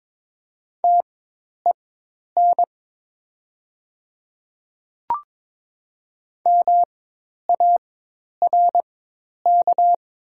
Morse code, heard as a keyed tone, about 22 wpm, spaced as for 10 wpm, 700 Hz. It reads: TEN MARK
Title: 4x Character Spacing - Top 500 Words - 22wpm